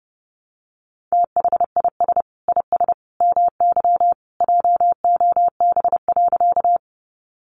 Morse code, heard as a keyed tone, about 30 wpm, 700 Hz. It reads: THIS IS MY JOB.
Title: Sentences from Top 600 Words 30wpm